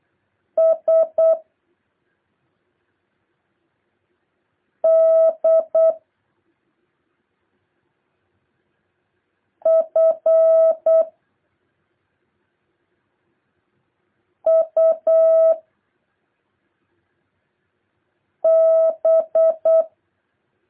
Morse code, heard as a keyed tone, about 8 wpm, 650 Hz. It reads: S D F U B
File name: jh8cbh / 2023cw